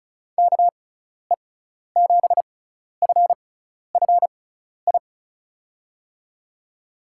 Morse code, heard as a keyed tone, about 34 wpm, spaced as for 12 wpm, 700 Hz. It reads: KE7FFI
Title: USA Call Signs - 6x Character Spacing - 35wpm